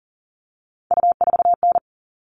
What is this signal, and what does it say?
Morse code, heard as a keyed tone, about 40 wpm, 700 Hz. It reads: U4N
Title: All Cut Numbers - 3-Digit 40wpm